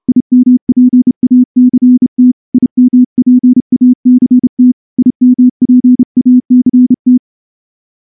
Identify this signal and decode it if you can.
Morse code, heard as a keyed tone, about 29 wpm, 250 Hz. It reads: IMPACT IMPACT IMPACT
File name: morse-training-30WPM